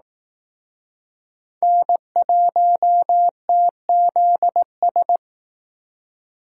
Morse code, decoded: N1TZS

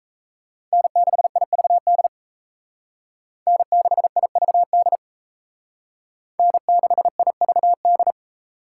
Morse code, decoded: N6IVD N6IVD N6IVD

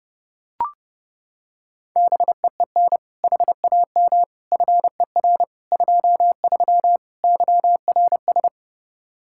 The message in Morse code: BEEN HAM FER 23 YRS

